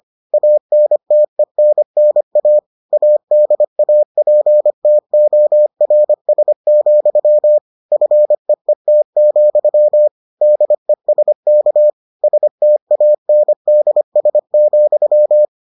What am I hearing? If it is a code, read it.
Morse code, heard as a keyed tone, about 25 wpm, 600 Hz. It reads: ANTENNA ADAPTORS, FEET, DESK STANDS,